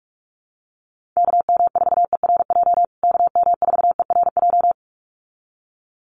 Morse code, heard as a keyed tone, about 44 wpm, 700 Hz. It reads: KM4ERJ KM4ERJ
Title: USA Call Signs Repeated 2x 45wpm